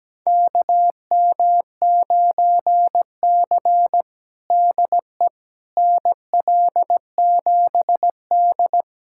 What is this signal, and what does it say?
Morse code, heard as a keyed tone, about 17 wpm, 700 Hz. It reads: KM9C DE NL7D
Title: USA Callsign DE Callsign - 4 Characters - 17wpm